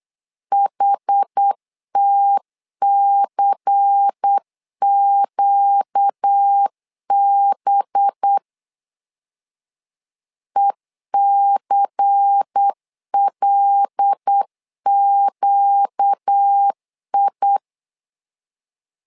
Morse code, heard as a keyed tone, about 8 wpm, 800 Hz. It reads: HTCQB ECLQI
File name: LEZIONI / 5 LBVC / 5 LBVC 8wpm 2935